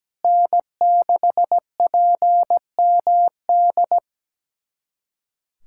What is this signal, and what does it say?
Morse code, heard as a keyed tone, about 17 wpm, 700 Hz. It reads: N6PMD
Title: USA Callsign DE Callsign - 5 Characters - 17wpm